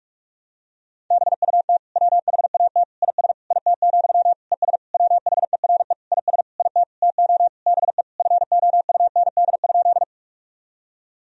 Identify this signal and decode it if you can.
Morse code, heard as a keyed tone, about 46 wpm, 700 Hz. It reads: BUT WHAT IS IT, ES WHERE IS IT TO BE FOUND?
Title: Sentences from Top 300 Words and Abbreviations 45wpm